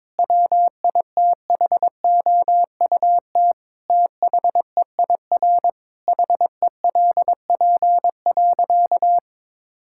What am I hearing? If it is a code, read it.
Morse code, heard as a keyed tone, about 22 wpm, 700 Hz. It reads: WITHOUT THEIR HELP.
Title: Sentences from Top 1000 Words - Review Part 6 - 22wpm